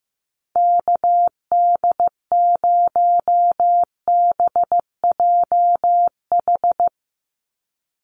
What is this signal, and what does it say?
Morse code, heard as a keyed tone, about 15 wpm, 700 Hz. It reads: KD0BJH
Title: USA Call Signs 15wpm